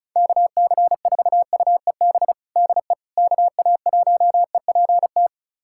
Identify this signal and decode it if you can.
Morse code, heard as a keyed tone, about 34 wpm, 700 Hz. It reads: KC4UEB DE KA1EPT
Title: USA Callsign DE Callsign - 6 Characters - 35wpm